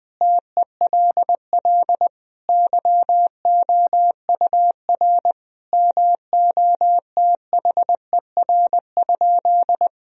Morse code, heard as a keyed tone, about 20 wpm, 700 Hz. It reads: TELL YOUR MOTHER?
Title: Sentences from Top 1000 Words - Review Part 4 - 20wpm